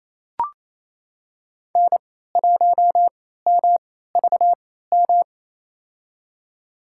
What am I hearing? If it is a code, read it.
Morse code, heard as a keyed tone, about 28 wpm, spaced as for 16 wpm, 700 Hz. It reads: N1MVM
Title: USA Call Signs - 3x Character Spacing - 28wpm